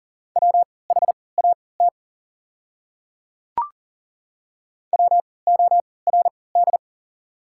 Morse code, heard as a keyed tone, about 40 wpm, spaced as for 23 wpm, 700 Hz. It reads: WHAT WORD
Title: 3x Character Spacing - Top 100 Words - 40wpm